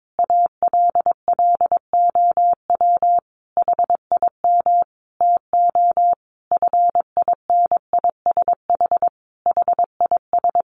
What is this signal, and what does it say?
Morse code, heard as a keyed tone, about 22 wpm, 700 Hz. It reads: ALLOW HIM TO FINISH HIS
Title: Sentences from Top 1200 Words 22wpm